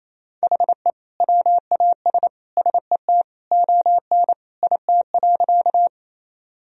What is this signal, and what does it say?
Morse code, heard as a keyed tone, about 28 wpm, 700 Hz. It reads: HE WAS SET ON IT.
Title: Sentences from Top 1000 Words - Review Part 1 - 28wpm